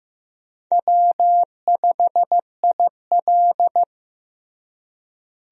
Morse code, heard as a keyed tone, about 15 wpm, 700 Hz. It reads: W5IL
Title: USA Call Signs - 4 Characters - Repeated 2x - 15wpm